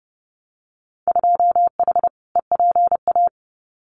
Morse code, heard as a keyed tone, about 30 wpm, 700 Hz. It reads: 2H EPA
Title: Winter Field Day Exchanges 30wpm